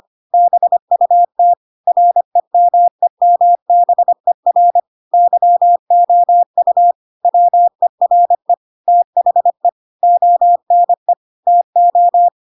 Morse code, heard as a keyed tone, about 25 wpm, 700 Hz. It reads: BUT REMEMBER YOU WERE THE ONE TO
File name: Pinocchio / Pinocchio ch2 25 25 ik1hge